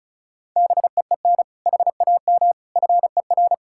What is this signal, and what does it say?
Morse code, heard as a keyed tone, about 36 wpm, 700 Hz. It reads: BEEN HAM FER
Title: K1USN SST QSO Practice 35wpm